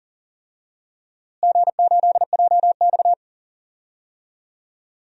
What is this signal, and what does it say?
Morse code, heard as a keyed tone, about 37 wpm, 700 Hz. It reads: G8JX
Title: DXCC-100 Call Signs - 4 Characters - Repeated 2x - 40wpm